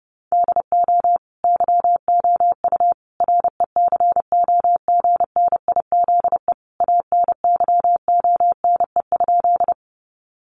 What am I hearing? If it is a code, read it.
Morse code, heard as a keyed tone, about 30 wpm, 700 Hz. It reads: DO YOU RECOGNIZE ANYONE?
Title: Sentences from Top 1400 Words 30wpm